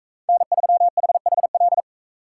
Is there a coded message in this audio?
Yes: Morse code from 0.3 to 1.8 s, heard as a keyed tone, about 42 wpm, 700 Hz. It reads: N3HHL